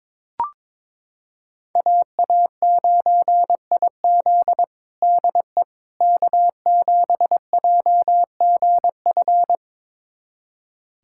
Morse code, heard as a keyed tone, about 22 wpm, 700 Hz. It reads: AA9IZ DE K7JGF
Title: USA Callsign DE Callsign - 5 Characters - 22wpm